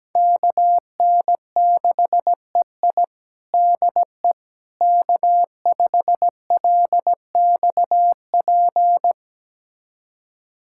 Morse code, heard as a keyed tone, about 17 wpm, 700 Hz. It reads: KN6EI DE K5LXP
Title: USA Callsign DE Callsign - 5 Characters - 17wpm